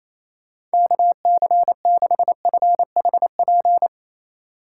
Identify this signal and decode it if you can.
Morse code, heard as a keyed tone, about 36 wpm, 700 Hz. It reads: KC6FHP